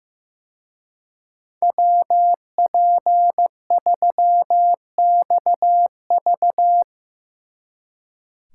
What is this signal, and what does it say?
Morse code, heard as a keyed tone, about 17 wpm, 700 Hz. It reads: WP3XV
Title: Speed Racing - USA 2x2 Call Signs 15wpm